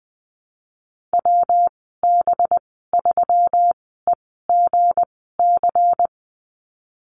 Morse code, heard as a keyed tone, about 19 wpm, spaced as for 14 wpm, 700 Hz. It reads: WB3EGC
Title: 200 Lesson 32g - Letter k - Call Signs 20wpm